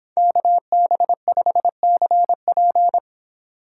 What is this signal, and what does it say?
Morse code, heard as a keyed tone, about 24 wpm, 700 Hz. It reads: KB5CP